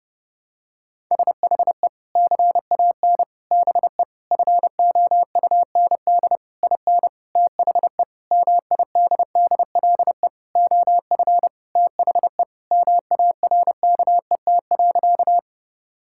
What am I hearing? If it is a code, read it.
Morse code, heard as a keyed tone, about 30 wpm, 700 Hz. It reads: SHE CAN BE FOUND IN THE MIDDLE OF THE MARKET.
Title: Sentences from Top 1000 Words - Review Part 6 - 30wpm